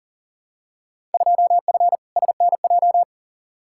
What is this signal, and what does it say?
Morse code, heard as a keyed tone, about 37 wpm, 700 Hz. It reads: 2F SNJ
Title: Field Day Exchanges Encore - 40wpm